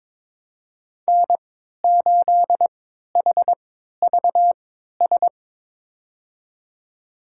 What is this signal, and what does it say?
Morse code, heard as a keyed tone, about 22 wpm, spaced as for 13 wpm, 700 Hz. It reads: N8HVS
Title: USA Call Signs - 5 Characters - 3x Character Spacing - 22wpm